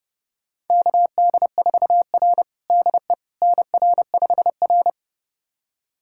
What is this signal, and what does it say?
Morse code, heard as a keyed tone, about 32 wpm, 700 Hz. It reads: KD4R DE NR5R
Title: USA Callsign DE Callsign - 4 Characters - 30wpm